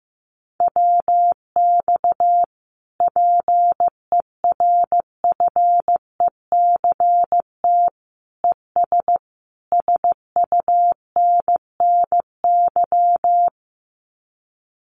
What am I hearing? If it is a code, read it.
Morse code, heard as a keyed tone, about 14 wpm, 700 Hz. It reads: WX PERFECT ES SUNNY
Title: Common QSO Phrases Encore 15wpm